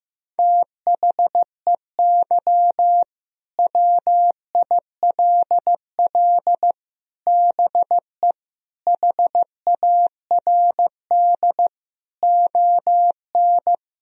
Morse code, heard as a keyed tone, about 15 wpm, 700 Hz. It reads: THEY WILL BE HARD ON
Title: Sentences from Top 1000 Words - Review Part 1 - 15wpm